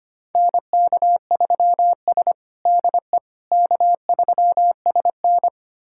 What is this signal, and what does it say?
Morse code, heard as a keyed tone, about 25 wpm, 700 Hz. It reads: NK3S DE K3SN
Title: USA Callsign DE Callsign - 4 Characters - 25wpm